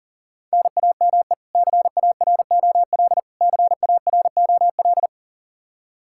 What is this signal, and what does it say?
Morse code, heard as a keyed tone, about 40 wpm, 700 Hz. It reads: NAME CAROL CAROL
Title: Practice QSO - Exchange 1 - Beginning - 40wpm